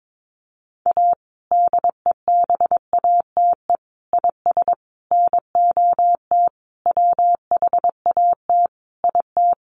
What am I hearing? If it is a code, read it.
Morse code, heard as a keyed tone, about 22 wpm, 700 Hz. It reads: A DEBATE IS NOT WHAT IT
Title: Sentences from Top 1100 Words 22wpm